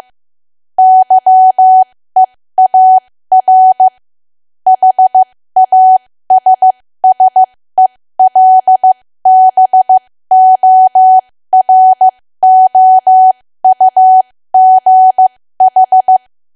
Morse code, heard as a keyed tone, about 15 wpm, spaced as for 13 wpm, 750 Hz. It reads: YEAR HASSELBOROUGH
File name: cwa / W1AW-Level2 / 13WPM